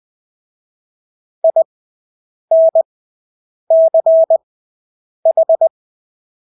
Morse code, heard as a keyed tone, about 20 wpm, spaced as for 8 wpm, 650 Hz. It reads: INCH
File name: Engelske ord 20/8 wpm